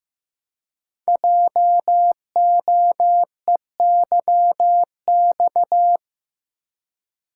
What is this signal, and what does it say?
Morse code, heard as a keyed tone, about 15 wpm, 700 Hz. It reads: JOEYX
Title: Code Groups - 5 Letter:Numbers - 15wpm